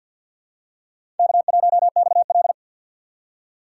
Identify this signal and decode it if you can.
Morse code, heard as a keyed tone, about 52 wpm, 700 Hz. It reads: K1XL
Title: USA Call Signs - 6x Character Spacing - 50wpm